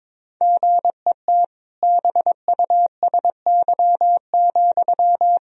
Morse code, heard as a keyed tone, about 22 wpm, 700 Hz. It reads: GET BUSY,
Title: Sentences from Top 1000 Words - Review Part 3 - 22wpm